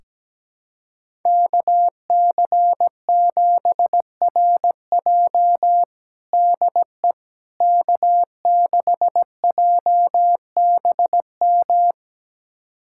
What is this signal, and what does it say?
Morse code, heard as a keyed tone, about 17 wpm, 700 Hz. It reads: KC7RJ DE K6JBM
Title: USA Callsign DE Callsign - 5 Characters - 17wpm